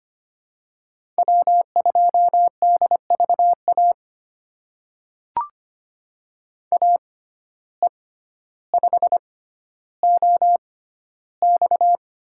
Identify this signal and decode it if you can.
Morse code, heard as a keyed tone, about 25 wpm, spaced as for 13 wpm, 700 Hz. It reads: W2DVA AE5OX
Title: USA Call Signs - 5 Characters - 6x Character Spacing - 25wpm